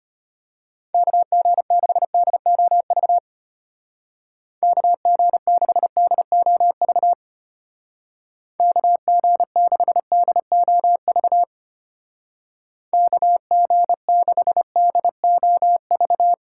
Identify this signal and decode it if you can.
Morse code, heard as a keyed tone, about 32 wpm, 700 Hz. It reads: KG6DOV KG6DOV KG6DOV KG6DOV